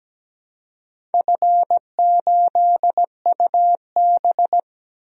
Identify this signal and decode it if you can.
Morse code, heard as a keyed tone, about 17 wpm, 700 Hz. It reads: F8UB
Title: DXCC-100 Call Signs - 4 Characters - Repeated 2x - 17wpm